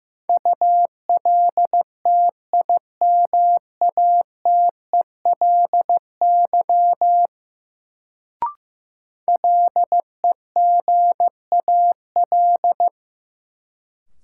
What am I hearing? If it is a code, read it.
Morse code, heard as a keyed tone, about 15 wpm, 700 Hz. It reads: ULTIMATELY LEGAL
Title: Sentences from Top 1200 Words - New Word Drill Alt - 15wpm